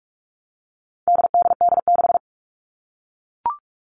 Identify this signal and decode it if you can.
Morse code, heard as a keyed tone, about 44 wpm, 700 Hz. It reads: DDD6
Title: All Cut Numbers - 4-Digit 45wpm